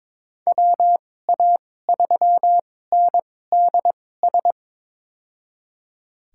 Morse code, heard as a keyed tone, about 22 wpm, spaced as for 16 wpm, 700 Hz. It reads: WA3NDS
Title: USA Call Signs - 6 Characters - 2x Character Spacing - 22wpm